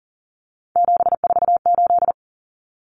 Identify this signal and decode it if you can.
Morse code, heard as a keyed tone, about 40 wpm, 700 Hz. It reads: 748
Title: Common Cut Numbers - 3-Digit 40wpm